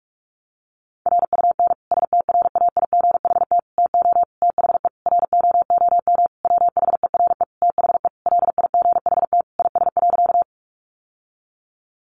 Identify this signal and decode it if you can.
Morse code, heard as a keyed tone, about 46 wpm, 700 Hz. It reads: RUN STRAIGHT TO THE ROOM WHERE THE LIGHT IS.
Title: Sentences from Top 1000 Words 45wpm